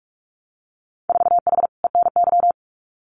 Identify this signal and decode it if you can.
Morse code, heard as a keyed tone, about 46 wpm, 700 Hz. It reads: 4H ENY
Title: Winter Field Day Exchanges Encore 45wpm